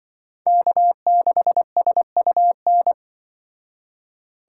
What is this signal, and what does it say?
Morse code, heard as a keyed tone, about 24 wpm, 700 Hz. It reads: K6SUN
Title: Speed Racing - USA 1x3 Call Signs 20wpm